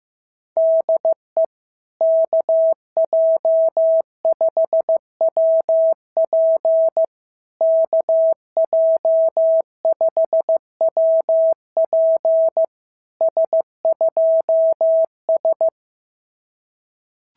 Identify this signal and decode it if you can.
Morse code, heard as a keyed tone, about 15 wpm, 650 Hz. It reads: DE KJ5WP KJ5WP S2S
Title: SOTA QSO Practice 15wpm